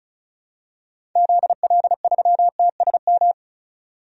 Morse code, heard as a keyed tone, about 36 wpm, 700 Hz. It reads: ZL3TSM
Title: DXCC-100 Call Signs - 6 Characters - 35wpm